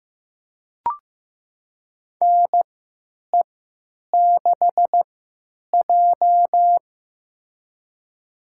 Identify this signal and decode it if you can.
Morse code, heard as a keyed tone, about 15 wpm, spaced as for 9 wpm, 700 Hz. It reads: NE6J
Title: USA Call Signs - 4 Characters - 3x Character Spacing - 15wpm